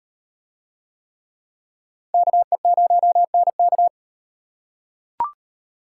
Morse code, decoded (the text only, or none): KE0NK